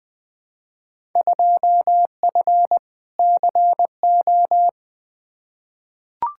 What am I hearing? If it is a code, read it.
Morse code, heard as a keyed tone, about 20 wpm, 700 Hz. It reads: 2F CO
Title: Field Day Exchanges - 20wpm